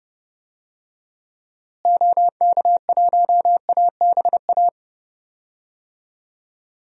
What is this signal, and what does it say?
Morse code, heard as a keyed tone, about 30 wpm, 700 Hz. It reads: OK1ABA
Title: DXCC-100 Call Signs - 6 Characters - Repeated 3x - 30wpm